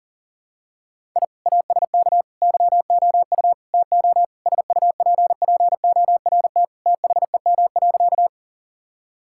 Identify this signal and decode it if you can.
Morse code, heard as a keyed tone, about 40 wpm, 700 Hz. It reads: I ASK YOU TO SUPPORT THEM.
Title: Sentences from Top 1000 Words - Review Part 5 - 40wpm